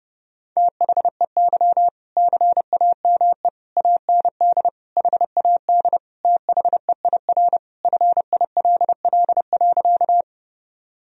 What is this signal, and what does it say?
Morse code, heard as a keyed tone, about 30 wpm, 700 Hz. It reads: THEY CAME AND HAD THEIR FILL.
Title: Sentences from Top 1000 Words - Review Part 3 - 30wpm